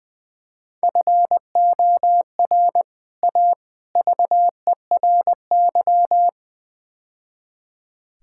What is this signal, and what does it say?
Morse code, heard as a keyed tone, about 20 wpm, 700 Hz. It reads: FOR A VERY
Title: Build A Phrase - 5-Word Phrases - 20wpm